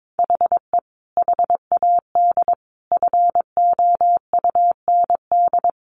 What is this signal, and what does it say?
Morse code, heard as a keyed tone, about 22 wpm, 700 Hz. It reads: HE HAD FOUND
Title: A Sound Of Thunder 22wpm